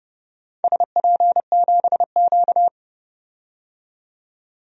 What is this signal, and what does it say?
Morse code, heard as a keyed tone, about 30 wpm, 700 Hz. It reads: SP7Q
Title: DXCC-100 Call Signs - 4 Characters - 30wpm